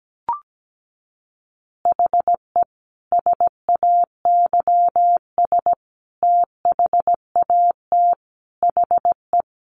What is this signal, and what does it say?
Morse code, heard as a keyed tone, about 17 wpm, 700 Hz. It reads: HE SAYS THAT HE
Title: Additional Easy Phrases 17wpm